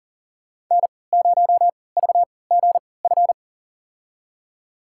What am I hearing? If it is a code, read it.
Morse code, heard as a keyed tone, about 40 wpm, spaced as for 23 wpm, 690 Hz. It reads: N0VGF
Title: USA Call Signs - 5 Characters - 3x Character Spacing - 40wpm